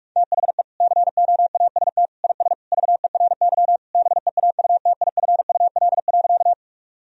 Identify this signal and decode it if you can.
Morse code, heard as a keyed tone, about 41 wpm, 700 Hz. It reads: THE COAST IS VERY BEAUTIFUL.
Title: Sentences from Top 1000 Words - Review Part 4 - 45wpm